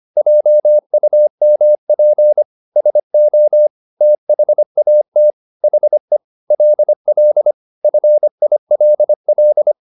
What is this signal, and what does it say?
Morse code, heard as a keyed tone, about 25 wpm, 600 Hz. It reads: JUMP SO THAT HE LL FILL
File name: the old man and the sea part 1 25wpm